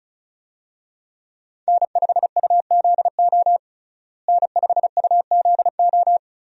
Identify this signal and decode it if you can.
Morse code, heard as a keyed tone, about 34 wpm, 700 Hz. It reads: N5UZO N5UZO
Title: USA Call Signs - 5 Characters - Repeated 2x - 35wpm